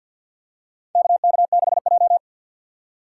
Morse code, heard as a keyed tone, about 50 wpm, 700 Hz. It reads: KK6J
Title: USA Call Signs - 4 Characters - 3x Character Spacing - 50wpm